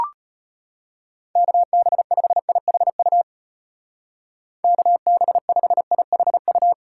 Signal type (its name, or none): Morse code